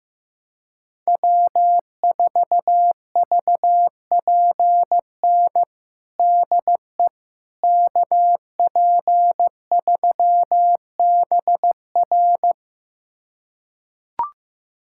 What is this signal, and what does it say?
Morse code, heard as a keyed tone, about 15 wpm, 700 Hz. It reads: W4VPN DE KP3BR
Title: USA Callsign DE Callsign - 5 Characters - 15wpm